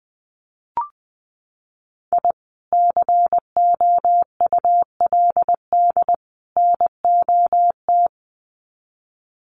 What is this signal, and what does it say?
Morse code, heard as a keyed tone, about 20 wpm, 700 Hz. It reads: I COULD NOT
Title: Sets of 3 Words 20wpm